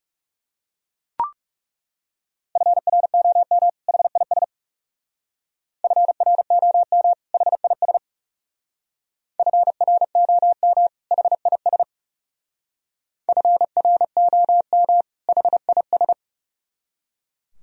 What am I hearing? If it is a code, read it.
Morse code, heard as a keyed tone, about 35 wpm, 700 Hz. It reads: FROM HIS FROM HIS FROM HIS FROM HIS
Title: Speed-Racing - Sets of 2 Words 30wpm